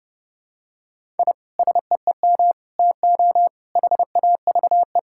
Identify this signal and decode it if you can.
Morse code, heard as a keyed tone, about 30 wpm, 700 Hz. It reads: I SEEM TO HAVE
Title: Sentences from Top 1000 Words - Review Part 5 - 30wpm